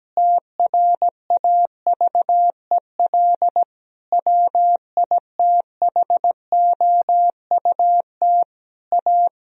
Morse code, heard as a keyed tone, about 17 wpm, 700 Hz. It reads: TRAVEL WITHOUT A
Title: Sentences from Top 1000 Words - Review Part 3 - 17wpm